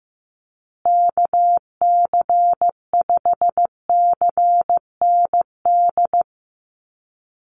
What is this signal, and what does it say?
Morse code, heard as a keyed tone, about 15 wpm, 700 Hz. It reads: KC5CND
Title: USA Call Signs Repeated 2x 15wpm